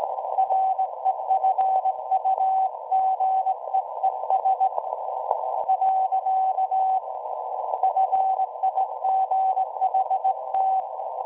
Morse code, heard as a keyed tone, about 19 wpm, 760 Hz. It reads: REFUGEES . FIGHT
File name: WebSDR records / SAQ at PI4THT part od 1005z 17-2kHz